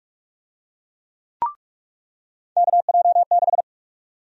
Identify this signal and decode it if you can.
Morse code, heard as a keyed tone, about 45 wpm, 700 Hz. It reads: KJ6
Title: USA Call Sign Prefixes - 45wpm